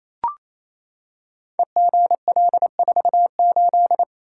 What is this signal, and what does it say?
Morse code, heard as a keyed tone, about 27 wpm, 700 Hz. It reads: EGL48
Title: Code Groups - 5 Letter:Numbers - 28wpm